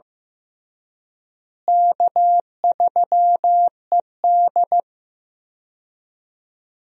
Morse code, heard as a keyed tone, about 16 wpm, 700 Hz. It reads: K3ED
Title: USA Call Signs - 4 Characters - Repeated 3x - 15wpm